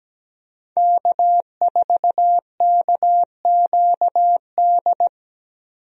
Morse code, heard as a keyed tone, about 17 wpm, 700 Hz. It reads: K4KQD
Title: USA Call Signs - 5 Characters - Repeated 3x - 17wpm